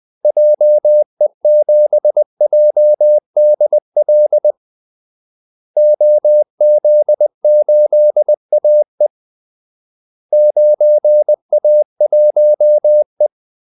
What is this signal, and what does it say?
Morse code, heard as a keyed tone, about 20 wpm, 600 Hz. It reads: JE7JDL OZ8AE 9A1E
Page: Kaldesignaler 20 wpm